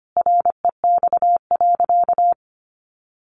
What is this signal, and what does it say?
Morse code, heard as a keyed tone, about 25 wpm, 700 Hz. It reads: REX.
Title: A Sound Of Thunder 25wpm